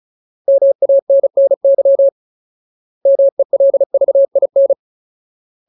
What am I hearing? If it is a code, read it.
Morse code, heard as a keyed tone, about 34 wpm, 550 Hz. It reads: MANNY MELVIN
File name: Ultimate Word List / 35WPM Ultimate